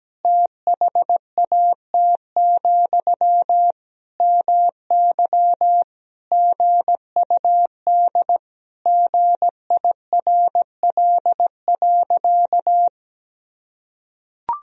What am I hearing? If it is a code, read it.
Morse code, heard as a keyed tone, about 17 wpm, 700 Hz. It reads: THAT, MY GUD GIRL.
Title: Sentences from Top 400 Words and Abbreviations 17wpm